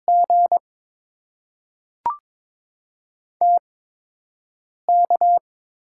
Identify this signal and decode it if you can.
Morse code, heard as a keyed tone, about 22 wpm, spaced as for 6 wpm, 700 Hz. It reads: G TK
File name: Top 90 QSO Elements - 8x Character Spacing - 22wpm